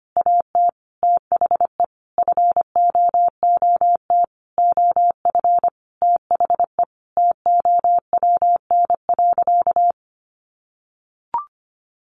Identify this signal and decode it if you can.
Morse code, heard as a keyed tone, about 25 wpm, 700 Hz. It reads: AT THE FOOT OF THE TOWN.
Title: Sentences from Top 1400 Words 25wpm